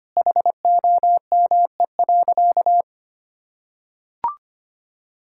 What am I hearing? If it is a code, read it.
Morse code, heard as a keyed tone, about 25 wpm, 700 Hz. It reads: HOME.
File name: practice / 25wpm / Sentences with 4-Word Phrases - 25wpm